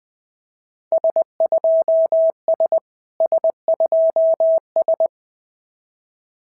Morse code, heard as a keyed tone, about 19 wpm, 690 Hz. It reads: S2S S2S
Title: SOTA QSO Practice 20wpm